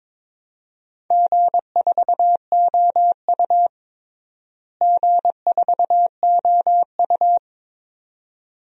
Morse code, heard as a keyed tone, about 22 wpm, 700 Hz. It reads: G4OU G4OU
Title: DXCC-100 Call Signs - 4 Characters - Repeated 2x - 22wpm